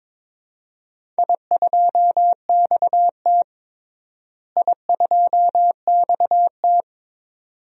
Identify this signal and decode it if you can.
Morse code, heard as a keyed tone, about 22 wpm, 700 Hz. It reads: I2XT I2XT
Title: DXCC-100 Call Signs - 4 Characters - Repeated 2x - 22wpm